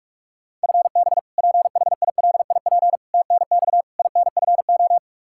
Morse code, heard as a keyed tone, about 46 wpm, 700 Hz. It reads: FB PHILIP TNX INFO
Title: Practice QSO - Exchange 3 - More Info - 45wpm